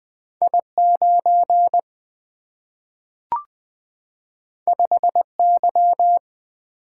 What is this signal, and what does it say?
Morse code, heard as a keyed tone, about 20 wpm, 700 Hz. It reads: I9 5Y
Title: Single-Letter Single-Number 20wpm